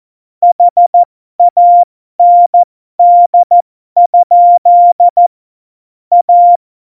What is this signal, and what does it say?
Morse code, heard as a keyed tone, about 14 wpm, spaced as for 12 wpm, 700 Hz. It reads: HAND? A
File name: Morsekurs / Witze